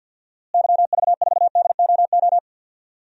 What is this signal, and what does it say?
Morse code, heard as a keyed tone, about 52 wpm, 700 Hz. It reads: YV4DOO
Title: DXCC-100 Call Signs - 6 Characters - Repeated 3x - 50wpm